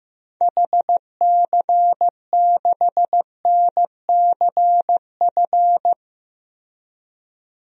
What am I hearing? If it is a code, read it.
Morse code, heard as a keyed tone, about 16 wpm, 700 Hz. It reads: HC6NCF
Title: DXCC-100 Call Signs - 6 Characters - Repeated 3x - 15wpm